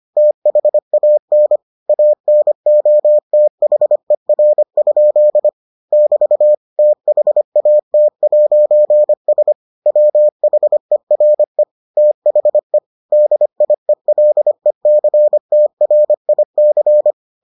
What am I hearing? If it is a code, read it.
Morse code, heard as a keyed tone, about 25 wpm, 600 Hz. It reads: THAN ANOTHER? <BT> THAT'S WHERE THE DIELECTRIC